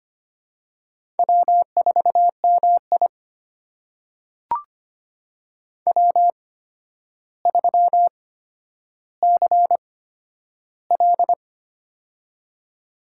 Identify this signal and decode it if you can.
Morse code, heard as a keyed tone, about 25 wpm, spaced as for 11 wpm, 700 Hz. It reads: W4MI W3CL
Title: USA Call Signs - 4 Characters - 8x Character Spacing - 25wpm